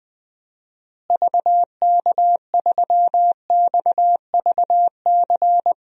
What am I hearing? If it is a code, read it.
Morse code, heard as a keyed tone, about 21 wpm, 700 Hz. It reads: VK3XVC
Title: DXCC-100 Call Signs - 6 Characters - Repeated 3x - 20wpm